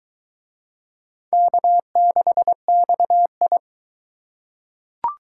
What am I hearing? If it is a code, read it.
Morse code, heard as a keyed tone, about 22 wpm, 700 Hz. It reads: K6XI